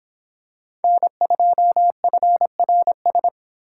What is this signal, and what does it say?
Morse code, heard as a keyed tone, about 25 wpm, 700 Hz. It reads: N2FRS